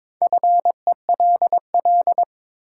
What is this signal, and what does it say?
Morse code, heard as a keyed tone, about 22 wpm, 700 Hz. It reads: FELL
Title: Sentences from Top 1600 Words - New Word Drill Alt - 22wpm